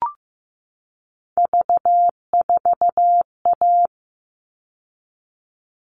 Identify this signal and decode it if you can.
Morse code, heard as a keyed tone, about 15 wpm, 700 Hz. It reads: V4A
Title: All Cut Numbers - 3-Digit 15wpm